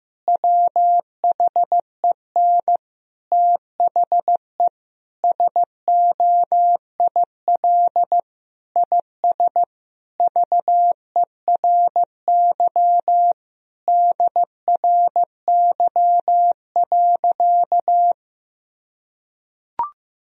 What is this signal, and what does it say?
Morse code, heard as a keyed tone, about 15 wpm, 700 Hz. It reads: WHEN THE SOIL IS VERY DRY.
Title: Sentences from Top 1000 Words - Review Part 4 - 15wpm